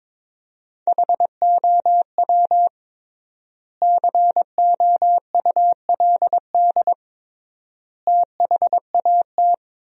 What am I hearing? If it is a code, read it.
Morse code, heard as a keyed tone, about 22 wpm, 700 Hz. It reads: HOW COULD THAT
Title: 3x Word-Spacing - Sentences from Top 400 Words - 22wpm